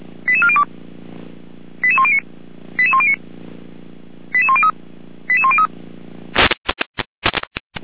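Five-tone selective call (ZVEI), heard as quick runs of five tones, tones of about 80 ms. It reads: 88402 88109 88109 88103 88103
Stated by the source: ZVEI five-tone selective call received by SDR, fire brigade Orlamünde (feuerwehr-orlamuende), address 88192.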